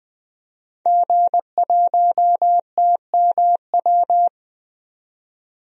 Morse code, heard as a keyed tone, about 20 wpm, 700 Hz. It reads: G1TMW
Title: DXCC-100 Call Signs - 5 Characters - 20wpm